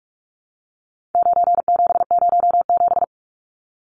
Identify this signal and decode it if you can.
Morse code, heard as a keyed tone, about 44 wpm, 700 Hz. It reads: 9707
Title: Numbers - 4-Digit 45wpm